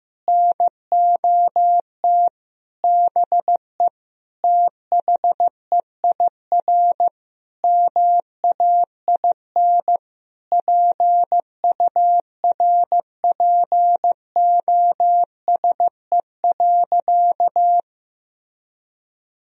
Morse code, heard as a keyed tone, about 15 wpm, 700 Hz. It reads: NOT BE THEIR MAIN PURPOSE.
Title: Sentences from Top 1000 Words - Review Part 2 - 15wpm